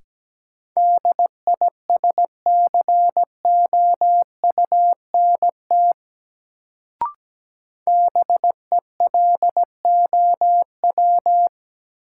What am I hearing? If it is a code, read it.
Morse code, heard as a keyed tone, about 17 wpm, 700 Hz. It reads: DISCOUNT BELOW
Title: Sentences from Top 1200 Words - New Word Drill Alt - 17wpm